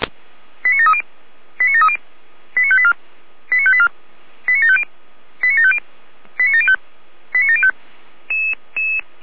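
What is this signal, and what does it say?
Five-tone selective call (ZVEI), heard as quick runs of five tones, tones of about 70 ms. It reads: 89730 89730 89584 89584 89750 89750 89705 89705
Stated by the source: ZVEI five-tone call received by SDR.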